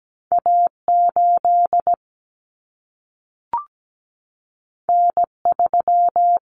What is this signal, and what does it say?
Morse code, heard as a keyed tone, about 17 wpm, 700 Hz. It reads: A8 N3
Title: Common Cut Numbers - 2-Digit 17wpm